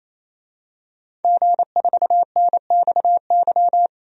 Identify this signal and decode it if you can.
Morse code, heard as a keyed tone, about 28 wpm, 700 Hz. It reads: G4NXY
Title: DXCC-100 Call Signs - 5 Characters - 28wpm